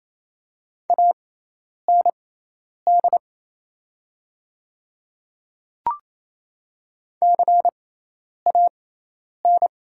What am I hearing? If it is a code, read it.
Morse code, heard as a keyed tone, about 28 wpm, spaced as for 10 wpm, 700 Hz. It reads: AND CAN
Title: 6x Character Spacing - Top 200 Words - 28wpm